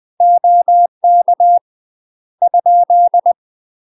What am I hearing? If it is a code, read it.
Morse code, heard as a keyed tone, about 20 wpm, 700 Hz. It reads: OK ?
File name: UK MASTER QSOs TRAINING / 20WPM / UK MASTER QSOs 0066